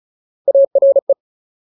Morse code, heard as a keyed tone, about 34 wpm, 550 Hz. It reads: ARE
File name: Ultimate Word List / 35WPM Ultimate